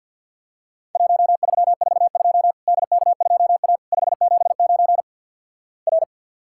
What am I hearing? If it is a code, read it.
Morse code, heard as a keyed tone, about 48 wpm, 700 Hz. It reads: 1342 DK2A 579 R
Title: Worked All Europe DX Contest - QTC Exchange - 50wpm